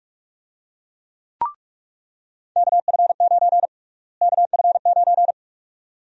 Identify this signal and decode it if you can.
Morse code, heard as a keyed tone, about 42 wpm, 700 Hz. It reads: KF9 KF9
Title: USA Call Sign Prefixes - Repeated 2x - 45wpm